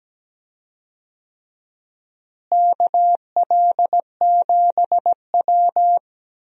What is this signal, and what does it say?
Morse code, heard as a keyed tone, about 17 wpm, 700 Hz. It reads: KL7W
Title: DXCC-100 Call Signs - 4 Characters - Repeated 3x - 17wpm